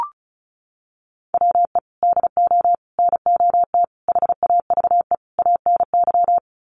Morse code, heard as a keyed tone, about 34 wpm, 700 Hz. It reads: WE DO NOT HAVE ANY
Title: Sets of 5 Words 35wpm